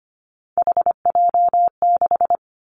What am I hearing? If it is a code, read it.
Morse code, heard as a keyed tone, about 25 wpm, 700 Hz. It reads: HJ6
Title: Callsign Prefixes for Top 100 DXCC - 25wpm